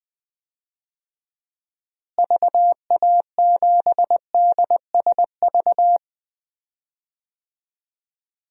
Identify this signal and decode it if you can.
Morse code, heard as a keyed tone, about 20 wpm, 700 Hz. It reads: VA7DSV
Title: DXCC-100 Call Signs - 6 Characters - Repeated 3x - 20wpm